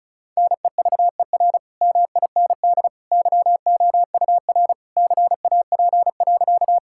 Morse code, heard as a keyed tone, about 36 wpm, 700 Hz. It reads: NEVER MIND YOUR CAP.
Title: Sentences from Top 1600 Words 35wpm